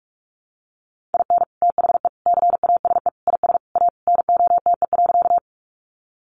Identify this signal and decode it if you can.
Morse code, heard as a keyed tone, about 44 wpm, 700 Hz. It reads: IN THE CASE IS A NOTE.
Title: Sentences from Top 700 Words 45wpm